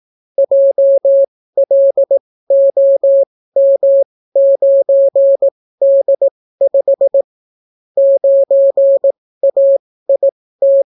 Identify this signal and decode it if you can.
Morse code, heard as a keyed tone, about 18 wpm, spaced as for 15 wpm, 550 Hz. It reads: JLOM9D5 9AIT